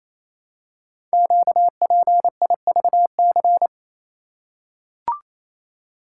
Morse code, heard as a keyed tone, about 27 wpm, 700 Hz. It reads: QPIVC
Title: Code Groups - 5 Letters - 28wpm